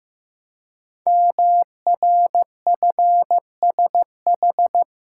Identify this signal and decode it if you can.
Morse code, heard as a keyed tone, about 15 wpm, 700 Hz. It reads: MRFSH
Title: Code Groups - 5 Letters - 15wpm